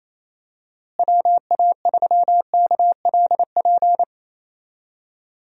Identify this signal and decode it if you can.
Morse code, heard as a keyed tone, about 28 wpm, 700 Hz. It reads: WA3KLP
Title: USA Call Signs - 6 Characters - Repeated 2x - 28wpm